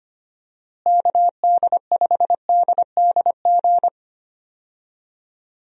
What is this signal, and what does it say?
Morse code, heard as a keyed tone, about 25 wpm, 700 Hz. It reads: KD5DDG
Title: USA Call Signs - 6 Characters - Repeated 2x - 25wpm